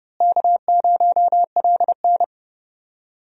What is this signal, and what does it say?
Morse code, heard as a keyed tone, about 30 wpm, 700 Hz. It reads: K0LN
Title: USA Call Signs - 4 Characters - Repeated 3x - 30wpm